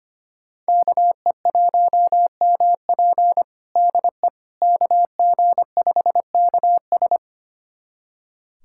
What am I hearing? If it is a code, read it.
Morse code, heard as a keyed tone, about 25 wpm, 700 Hz. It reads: KE1MP DE KG5KS
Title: USA Callsign DE Callsign - 5 Characters - 25wpm